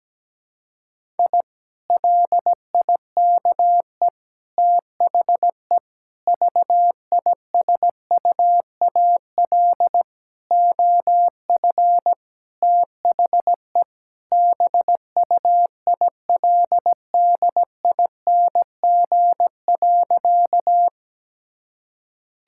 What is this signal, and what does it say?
Morse code, heard as a keyed tone, about 17 wpm, 700 Hz. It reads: I LIKE THE VISUAL OF THE BUILDING.
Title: Sentences from Top 1000 Words - Review Part 6 - 17wpm